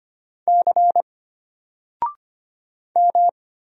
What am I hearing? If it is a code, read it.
Morse code, heard as a keyed tone, about 25 wpm, 700 Hz. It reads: C M
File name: Single-Letter-Number-25wpm